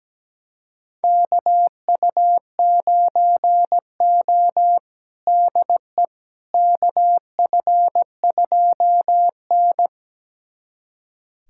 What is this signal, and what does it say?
Morse code, heard as a keyed tone, about 17 wpm, 700 Hz. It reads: KU9O DE KF2N
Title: USA Callsign DE Callsign - 4 Characters - 17wpm